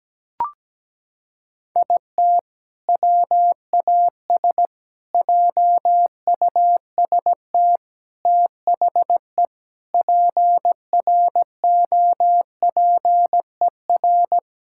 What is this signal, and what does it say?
Morse code, heard as a keyed tone, about 17 wpm, 700 Hz. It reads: IT WAS JUST THE PROPER
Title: Sentences from Top 1000 Words - Review Part 5 - 17wpm